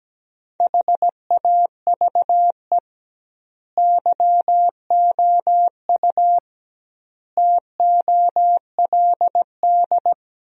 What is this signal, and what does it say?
Morse code, heard as a keyed tone, about 17 wpm, 700 Hz. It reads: HAVE YOU TOLD
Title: 2x Word-Spacing - Sentences from Top 500 Words - 17wpm